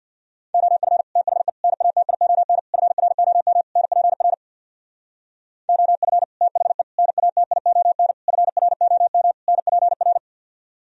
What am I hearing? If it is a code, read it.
Morse code, heard as a keyed tone, about 64 wpm, 700 Hz. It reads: OF THE NATION FROM NPR OF THE NATION FROM NPR